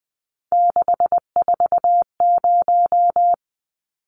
Morse code, heard as a keyed tone, about 20 wpm, 700 Hz. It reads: 640
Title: Numbers - 3-Digit 20wpm